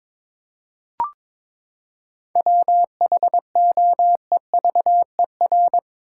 Morse code, heard as a keyed tone, about 22 wpm, 700 Hz. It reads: WHOEVER
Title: Seven Letter Words - 22wpm